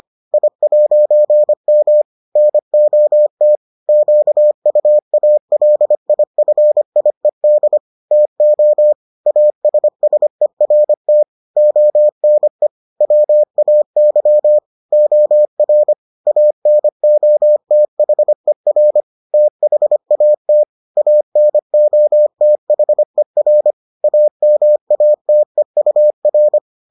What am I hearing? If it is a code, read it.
Morse code, heard as a keyed tone, about 25 wpm, 600 Hz. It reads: I'M NOT QUALIFIED TO ASSERT ONE WAY OR ANOTHER THAT ANOTHER AMATEUR